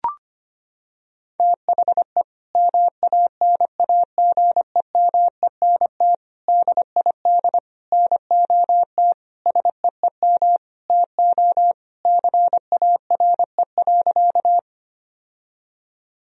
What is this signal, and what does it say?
Morse code, heard as a keyed tone, about 25 wpm, 700 Hz. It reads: THE MANAGEMENT DID NOT SEEM TO CARE.
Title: Sentences from Top 1000 Words - Review Part 4 - 25wpm